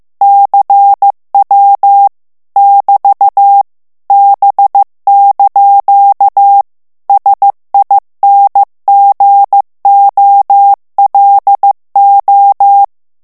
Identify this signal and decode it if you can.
Morse code, heard as a keyed tone, about 15 wpm, 800 Hz. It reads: CW <BT> B) SINGOLO
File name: morse mp3 / ITALIEN texte clair 2 vitesse 14 mots minute